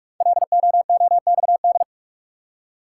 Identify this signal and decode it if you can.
Morse code, heard as a keyed tone, about 44 wpm, 700 Hz. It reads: LOOXD
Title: Code Groups - 5 Letter:Numbers - 45wpm